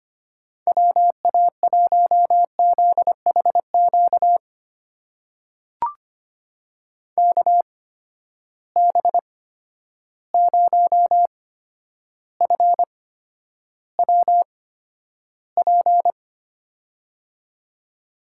Morse code, heard as a keyed tone, about 25 wpm, spaced as for 11 wpm, 700 Hz. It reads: WA1ZHQ KB0FWP
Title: USA Call Signs - 6 Characters - 8x Character Spacing - 25wpm